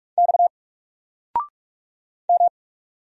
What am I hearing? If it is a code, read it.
Morse code, heard as a keyed tone, about 44 wpm, 700 Hz. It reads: X M
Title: Single Letter-Number 45wpm